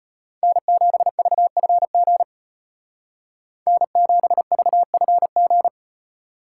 Morse code, heard as a keyed tone, about 31 wpm, 700 Hz. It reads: N7VFG N7VFG
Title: Speed Racing - USA Call Signs 25wpm